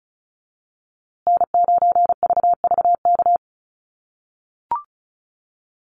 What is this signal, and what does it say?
Morse code, heard as a keyed tone, about 34 wpm, 700 Hz. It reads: N9VVK
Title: USA Call Signs Repeated 2x 35wpm